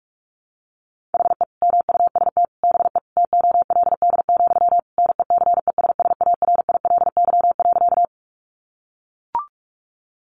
Morse code, heard as a keyed tone, about 44 wpm, 700 Hz. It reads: HE MUST BE TOLD, NECESSARILY.
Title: Sentences from Top 1400 Words 45wpm